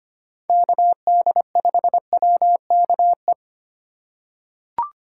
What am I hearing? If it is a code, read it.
Morse code, heard as a keyed tone, about 25 wpm, 700 Hz. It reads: KD5WKE
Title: USA Call Signs - 2x Character Spacing - 25wpm